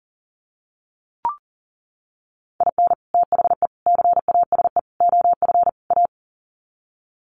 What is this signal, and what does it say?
Morse code, heard as a keyed tone, about 37 wpm, 700 Hz. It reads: IN THE CASE OF A
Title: Sets of 5 Words Encore 40wpm